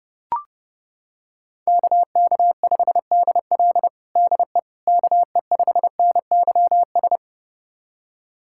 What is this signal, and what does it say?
Morse code, heard as a keyed tone, about 30 wpm, 700 Hz. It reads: KK5DL DE KE5NYS
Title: USA Callsign DE Callsign 30wpm